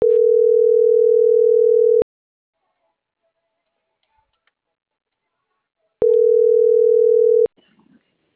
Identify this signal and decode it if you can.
Ringback tone (North America), 440 + 480 Hz.